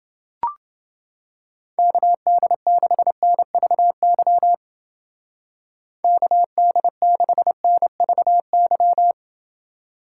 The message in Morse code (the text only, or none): KD6NVY KD6NVY